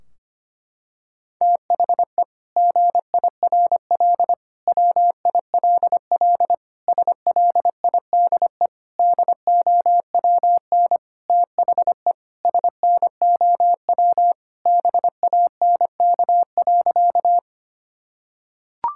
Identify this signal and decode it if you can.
Morse code, heard as a keyed tone, about 25 wpm, 700 Hz. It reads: THE GIRL WILL SLIDE DOWN THE SNOW BANK.
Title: Sentences from Top 1600 Words 25wpm